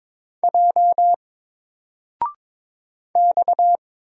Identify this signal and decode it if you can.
Morse code, heard as a keyed tone, about 22 wpm, 700 Hz. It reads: J X
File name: Single-Letter-Number-22wpm